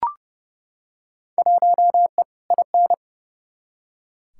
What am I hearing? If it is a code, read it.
Morse code, heard as a keyed tone, about 30 wpm, 700 Hz. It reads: 1E IN
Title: Field Day Exchanges - 30wpm